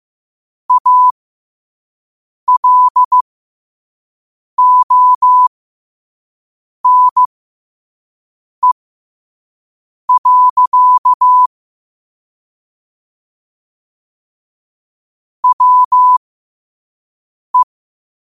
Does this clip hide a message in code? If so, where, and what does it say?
Yes: Morse code from 0.7 to 17.6 s, heard as a keyed tone, about 15 wpm, spaced as for 5 wpm, 1000 Hz. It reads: ALONE. WE